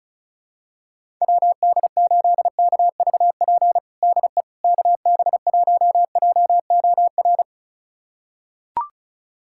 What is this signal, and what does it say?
Morse code, heard as a keyed tone, about 34 wpm, 700 Hz. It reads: WD8KVP DE KB1JOR
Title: USA Callsign DE Callsign - 6 Characters - 35wpm